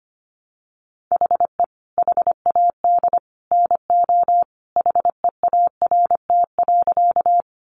Morse code, heard as a keyed tone, about 25 wpm, 700 Hz. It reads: HE HAD NO HEART.
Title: Sentences from Top 1200 Words 25wpm